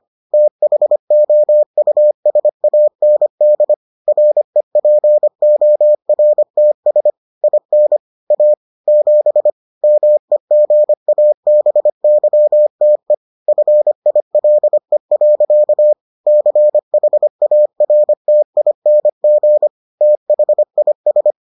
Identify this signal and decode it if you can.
Morse code, heard as a keyed tone, about 25 wpm, 600 Hz. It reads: THOUSAND REPORTS IN A 7 MEGABYTE FILE. CHARTING THIS